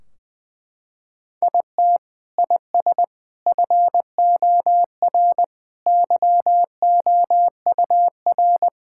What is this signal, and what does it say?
Morse code, heard as a keyed tone, about 20 wpm, 700 Hz. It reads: IT IS FOR YOUR
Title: Sentences from Top 1600 Words 20wpm